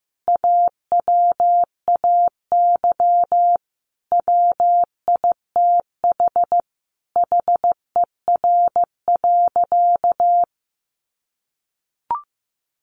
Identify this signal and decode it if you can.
Morse code, heard as a keyed tone, about 15 wpm, 700 Hz. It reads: AWAY WITH HER.
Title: Sentences from Top 1500 Words 15wpm